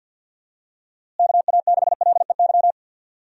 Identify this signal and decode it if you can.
Morse code, heard as a keyed tone, about 48 wpm, 700 Hz. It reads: KA6LEY
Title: USA Call Signs - 6 Characters - Repeated 2x - 50wpm